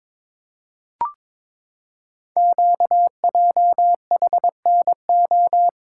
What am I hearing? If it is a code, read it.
Morse code, heard as a keyed tone, about 22 wpm, 700 Hz. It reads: QJHNO
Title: Code Groups - 5 Letters - 22wpm